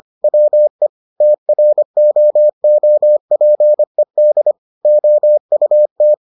Morse code, heard as a keyed tone, about 25 wpm, 600 Hz. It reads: WE TROOPED OUT